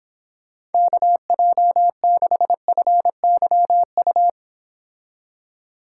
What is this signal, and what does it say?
Morse code, heard as a keyed tone, about 29 wpm, 700 Hz. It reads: KJ6FYU